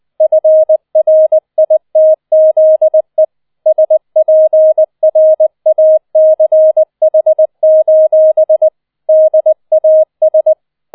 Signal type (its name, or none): Morse code